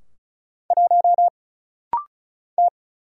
Morse code, heard as a keyed tone, about 36 wpm, 700 Hz. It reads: 1 T
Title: Single Letter-Number - Rapid-Fire 35wpm